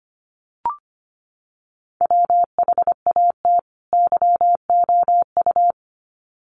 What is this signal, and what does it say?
Morse code, heard as a keyed tone, about 25 wpm, 700 Hz. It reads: WHAT YOU
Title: Sets of 2 Words 25wpm